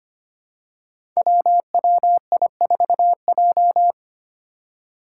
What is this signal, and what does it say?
Morse code, heard as a keyed tone, about 25 wpm, 700 Hz. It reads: WWI4J
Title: Five Challenging Letter-Number Combinations 25wpm